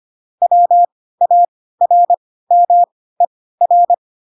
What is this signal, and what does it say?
Morse code, heard as a keyed tone, about 25 wpm, spaced as for 16 wpm, 700 Hz. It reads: WARMER